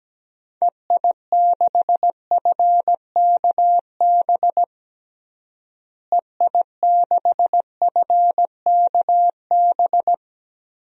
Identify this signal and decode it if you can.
Morse code, heard as a keyed tone, about 17 wpm, 700 Hz. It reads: EI6FKB EI6FKB
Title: DXCC-100 Call Signs - 6 Characters - Repeated 2x - 17wpm